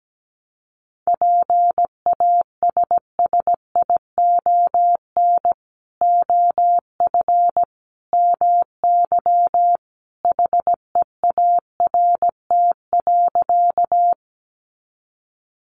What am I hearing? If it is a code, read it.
Morse code, heard as a keyed tone, about 17 wpm, 700 Hz. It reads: PASSION OF MY HEART.
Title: Sentences from Top 1300 Words 17wpm